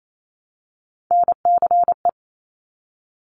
Morse code, heard as a keyed tone, about 28 wpm, 700 Hz. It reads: NCE
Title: Top Three Letter Combinations in Words 28wpm